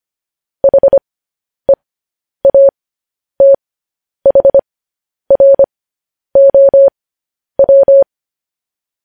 Morse code, heard as a keyed tone, about 25 wpm, spaced as for 10 wpm, 550 Hz. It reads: HEATHROW
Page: News Headlines in Morse Code at 10 WPM